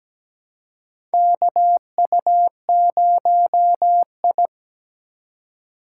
Morse code, heard as a keyed tone, about 17 wpm, 700 Hz. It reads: KU0I
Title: USA Call Signs - 4 Characters - 3x Character Spacing - 17wpm